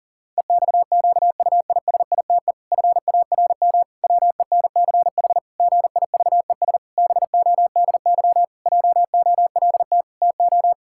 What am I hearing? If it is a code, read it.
Morse code, heard as a keyed tone, about 40 wpm, 700 Hz. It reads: EXQUISITE FARM WENCH GIVES BODY JOLT TO
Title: Pangram Challenge v2 - 40wpm